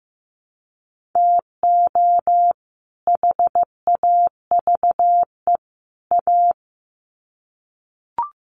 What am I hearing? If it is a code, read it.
Morse code, heard as a keyed tone, about 15 wpm, 700 Hz. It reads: TO HAVE A
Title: Sets of 3 Words 15wpm